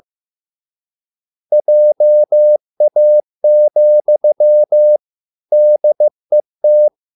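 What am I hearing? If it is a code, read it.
Morse code, heard as a keyed tone, about 15 wpm, 600 Hz. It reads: JA, DET
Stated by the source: Morse code at a slow 15 wpm from CW med Gnister.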